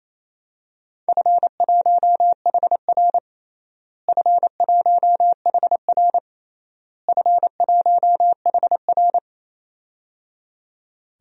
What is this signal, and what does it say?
Morse code, heard as a keyed tone, about 28 wpm, 700 Hz. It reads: F1HR F1HR F1HR
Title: DXCC-100 Call Signs - 4 Characters - Repeated 3x - 28wpm